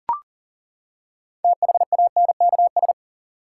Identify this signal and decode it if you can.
Morse code, heard as a keyed tone, about 40 wpm, 700 Hz. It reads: THANKS
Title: Six Letter Words - 40wpm